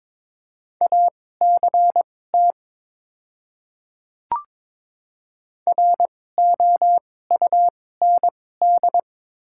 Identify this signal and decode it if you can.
Morse code, heard as a keyed tone, about 22 wpm, spaced as for 16 wpm, 700 Hz. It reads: ACT ROUND
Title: 2x Character Spacing - Top 500 Words - 22wpm